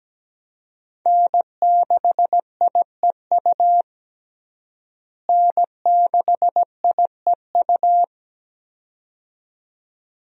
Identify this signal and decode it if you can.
Morse code, heard as a keyed tone, about 17 wpm, 700 Hz. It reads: N6IEU N6IEU
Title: USA Call Signs - 5 Characters - Repeated 2x - 17wpm